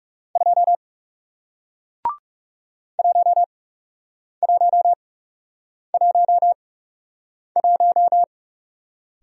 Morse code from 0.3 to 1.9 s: 2

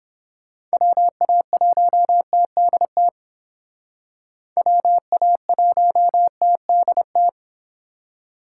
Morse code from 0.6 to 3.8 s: WA1TDT

